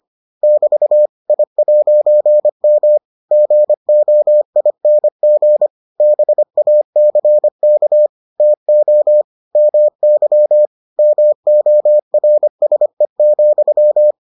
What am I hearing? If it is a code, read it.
Morse code, heard as a keyed tone, about 25 wpm, 600 Hz. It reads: <BT> I'M GOING BACK TO MY MORSE,